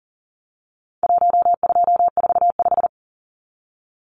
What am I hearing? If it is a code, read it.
Morse code, heard as a keyed tone, about 40 wpm, 700 Hz. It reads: 1245
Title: Numbers - Contest Style - 4-Digit 40wpm